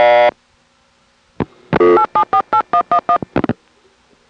DTMF, heard as telephone touch tones, tones of about 90 ms. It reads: 8748111